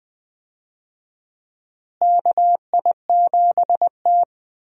Morse code, heard as a keyed tone, about 20 wpm, 700 Hz. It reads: KI7T
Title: USA Call Signs - 4 Characters - Repeated 3x - 20wpm